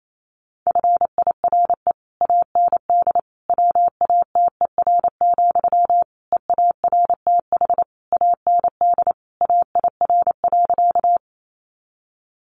Morse code, heard as a keyed tone, about 28 wpm, 700 Hz. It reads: FIRE AND WATER, EARTH AND AIR.
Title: Sentences from Top 400 Words 28wpm